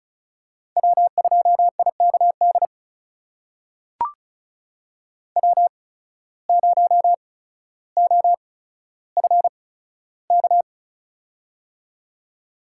Morse code, heard as a keyed tone, about 36 wpm, spaced as for 15 wpm, 700 Hz. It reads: W2IKD W0OFK
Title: USA Call Signs - 5 Characters - 8x Character Spacing - 35wpm